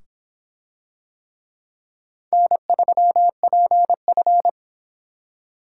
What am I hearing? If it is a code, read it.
Morse code, heard as a keyed tone, about 25 wpm, 700 Hz. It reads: N3PF